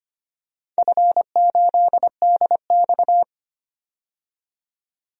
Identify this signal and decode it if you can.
Morse code, heard as a keyed tone, about 25 wpm, 690 Hz. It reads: F8DX
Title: DXCC-100 Call Signs - 4 Characters - 25wpm